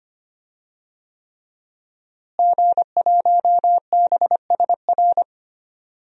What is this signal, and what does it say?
Morse code, heard as a keyed tone, about 25 wpm, 700 Hz. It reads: G1BSR
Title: DXCC-100 Call Signs - 5 Characters - Repeated 3x - 25wpm